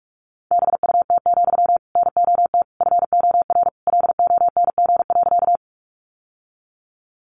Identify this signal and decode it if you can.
Morse code, heard as a keyed tone, about 44 wpm, 700 Hz. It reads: BUT, NOT FOR LONG.
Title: Sentences from Top 600 Words 45wpm